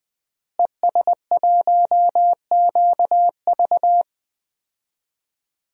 Morse code, heard as a keyed tone, about 20 wpm, 700 Hz. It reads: ES1QV
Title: DXCC-100 Call Signs - 5 Characters - Repeated 3x - 20wpm